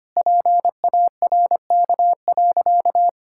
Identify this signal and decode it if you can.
Morse code, heard as a keyed tone, about 25 wpm, 700 Hz. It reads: PARK.
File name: practice / 25wpm / Sentences with 4-Word Phrases - 25wpm